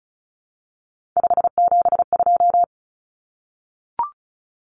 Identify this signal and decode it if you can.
Morse code, heard as a keyed tone, about 34 wpm, 700 Hz. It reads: 572
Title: Common Cut Numbers - 3-Digit 35wpm